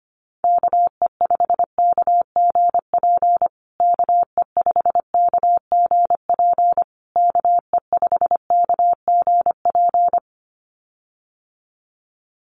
Morse code, heard as a keyed tone, about 25 wpm, 700 Hz. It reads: KE5KGP KE5KGP KE5KGP
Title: USA Call Signs Repeated 3x 25wpm